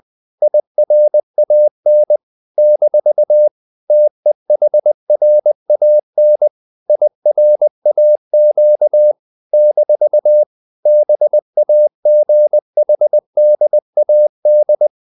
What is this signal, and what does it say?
Morse code, heard as a keyed tone, about 20 wpm, 600 Hz. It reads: IRAN - TEHRAN IRAQ - BAGHDAD
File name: CW / Practice / NationalCapitals20